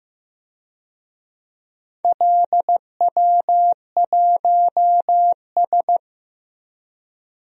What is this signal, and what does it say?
Morse code, heard as a keyed tone, about 15 wpm, 700 Hz. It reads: LW1S
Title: DXCC-100 Call Signs - 4 Characters - Repeated 2x - 15wpm